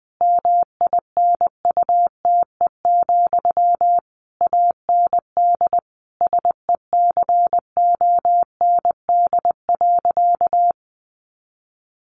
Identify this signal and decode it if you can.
Morse code, heard as a keyed tone, about 20 wpm, 700 Hz. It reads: MINUTE, AND SECOND.
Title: A Sound Of Thunder 20wpm